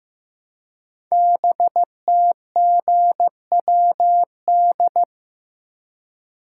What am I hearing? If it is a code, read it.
Morse code, heard as a keyed tone, about 15 wpm, 700 Hz. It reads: BTGWD
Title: Code Groups - 5 Letters - 15wpm